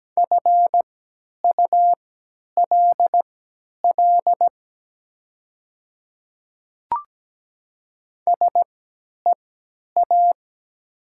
Morse code, heard as a keyed tone, about 17 wpm, spaced as for 10 wpm, 700 Hz. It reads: FULL SEA
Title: 3x Character Spacing - Top 500 Words - 17wpm